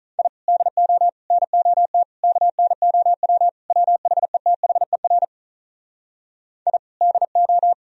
Morse code, heard as a keyed tone, about 46 wpm, 700 Hz. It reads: I DO NOT KNOW WHETHER I DO